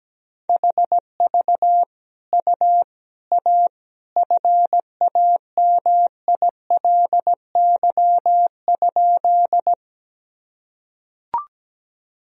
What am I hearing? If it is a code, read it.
Morse code, heard as a keyed tone, about 17 wpm, 700 Hz. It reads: HV U A FAMILY?
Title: Sentences from Top 400 Words and Abbreviations 17wpm